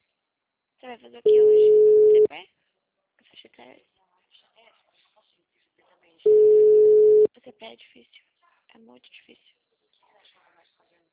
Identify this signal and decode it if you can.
Ringback tone (Europe, 425 Hz), with speech.
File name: out- -2055- - - .4351